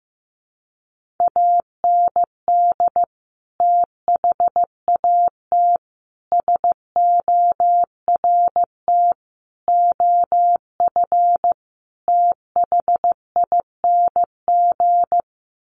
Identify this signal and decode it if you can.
Morse code, heard as a keyed tone, about 15 wpm, 700 Hz. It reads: AND THAT SORT OF THING
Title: Sets of 5 Words Encore2 15wpm